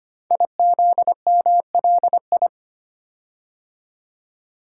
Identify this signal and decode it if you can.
Morse code, heard as a keyed tone, about 25 wpm, 700 Hz. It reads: IZMLI
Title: Code Groups - 5 Letters - 25wpm